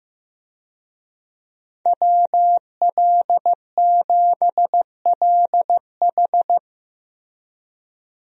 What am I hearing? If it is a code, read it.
Morse code, heard as a keyed tone, about 15 wpm, 700 Hz. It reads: WL7LH
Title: USA Call Signs - 5 Characters - Repeated 2x - 15wpm